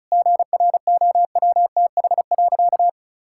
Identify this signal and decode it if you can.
Morse code, heard as a keyed tone, about 36 wpm, 700 Hz. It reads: GROWTH.
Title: Sentences from Top 1000 Words - Review Part 5 - 35wpm